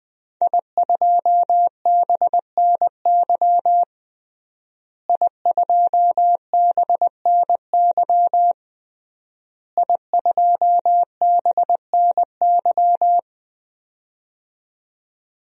Morse code, heard as a keyed tone, about 21 wpm, 700 Hz. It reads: I2BNY I2BNY I2BNY
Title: DXCC-100 Call Signs - 5 Characters - Repeated 3x - 20wpm